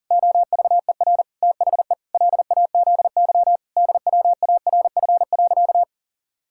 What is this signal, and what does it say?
Morse code, heard as a keyed tone, about 40 wpm, 700 Hz. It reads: OVER THE LAZY DWARF.
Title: Pangram Challenge v2 - 40wpm